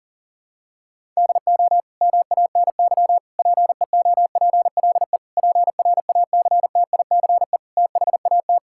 Morse code, heard as a keyed tone, about 41 wpm, 700 Hz. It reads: DO MANY PEOPLE PRACTICE THAT
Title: Sentences from Top 1000 Words - Review Part 6 - 40wpm